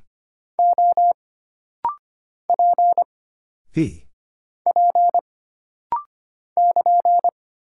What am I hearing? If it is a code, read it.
Morse code, heard as a keyed tone, about 25 wpm, 700 Hz. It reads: O P P <KN>
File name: Single Character (Letter:Number:Punctuation:Prosign) - Rapid-Fire - 25wpm